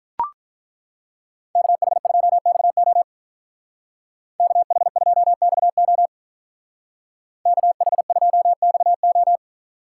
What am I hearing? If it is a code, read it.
Morse code, heard as a keyed tone, about 43 wpm, 700 Hz. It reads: KH2XO KH2XO KH2XO
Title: Speed Racing - USA 2x2 Call Signs 35wpm